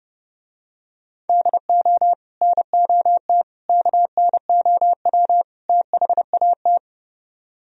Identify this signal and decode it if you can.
Morse code, heard as a keyed tone, about 30 wpm, 700 Hz. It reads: DO NOT KNOW THAT